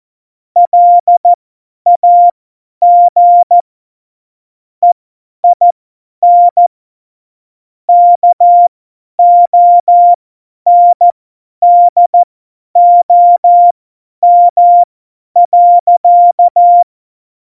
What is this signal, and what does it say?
Morse code, heard as a keyed tone, about 14 wpm, spaced as for 10 wpm, 700 Hz. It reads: LAG EIN KONDOM.
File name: Morsekurs / Witze / Witz 12. Januar 2021, 50 F70